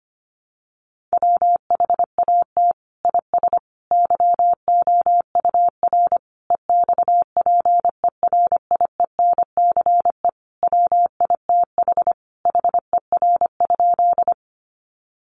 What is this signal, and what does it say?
Morse code, heard as a keyed tone, about 25 wpm, 700 Hz. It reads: WHAT IS YOUR EXPERIENCE WITH HER?
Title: Sentences from Top 600 Words 25wpm